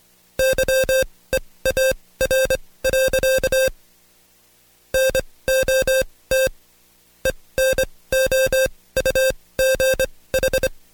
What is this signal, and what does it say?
Morse code, heard as a keyed tone, about 24 wpm, spaced as for 18 wpm, 530 Hz. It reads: YEAR. NOT ENOUGH
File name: Code Quick / AudioDownld2 / Seg135-Stepper8 17WPM